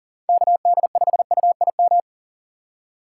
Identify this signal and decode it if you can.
Morse code, heard as a keyed tone, about 38 wpm, 700 Hz. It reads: KD5UIM
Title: USA Call Signs - 6 Characters - 6x Character Spacing - 40wpm